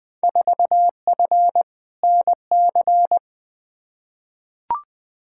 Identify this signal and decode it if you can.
Morse code, heard as a keyed tone, about 20 wpm, 700 Hz. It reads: 4F NC
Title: Field Day Exchanges - 20wpm